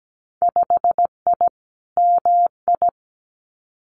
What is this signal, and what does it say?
Morse code, heard as a keyed tone, about 17 wpm, 700 Hz. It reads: 5I MI
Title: Winter Field Day Exchanges Encore 17wpm